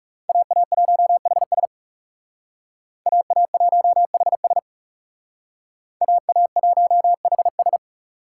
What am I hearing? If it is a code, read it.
Morse code, heard as a keyed tone, about 40 wpm, 700 Hz. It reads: AA1HS AA1HS AA1HS